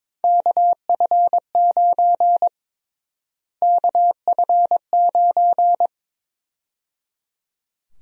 Morse code, heard as a keyed tone, about 22 wpm, 700 Hz. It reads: KF9 KF9
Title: USA Call Sign Prefixes - Repeated 2x - 22wpm